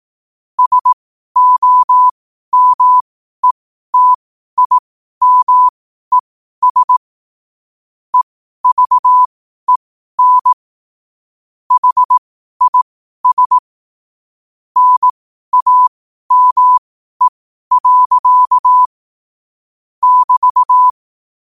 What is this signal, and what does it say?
Morse code, heard as a keyed tone, about 18 wpm, spaced as for 12 wpm, 1000 Hz. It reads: SOMETIMES EVEN HIS NAME. <BT>
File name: qotdfiles / 12wpm